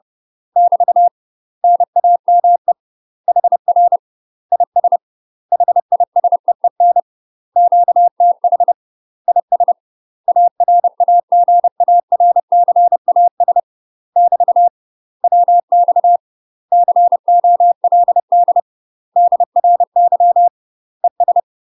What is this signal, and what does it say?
Morse code, heard as a keyed tone, about 30 wpm, 700 Hz. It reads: <BT> NAME HR IS HISEEN QTH IS ARAGARCAS <BT> WX COLD DRY ES
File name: UK MASTER QSOs TRAINING / 30WPM / UK MASTER QSOs 0035